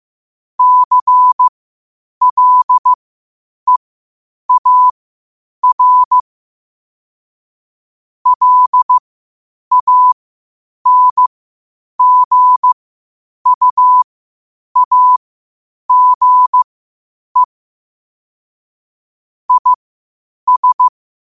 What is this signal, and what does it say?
Morse code, heard as a keyed tone, about 15 wpm, spaced as for 8 wpm, 1000 Hz. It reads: CLEAR LANGUAGE IS